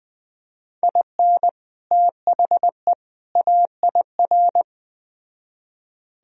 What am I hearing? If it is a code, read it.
Morse code, heard as a keyed tone, about 20 wpm, 700 Hz. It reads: IN THE AIR
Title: Speed-Racing - Sets of 3 Words 17wpm